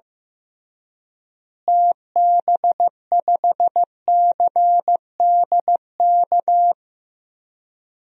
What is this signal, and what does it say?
Morse code, heard as a keyed tone, about 15 wpm, 700 Hz. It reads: TB5CDK
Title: DXCC-100 Call Signs - 6 Characters - Repeated 2x - 15wpm